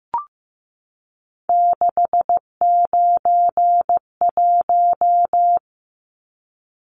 Morse code, heard as a keyed tone, about 15 wpm, 700 Hz. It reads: 691
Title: Numbers - 3-Digit 15wpm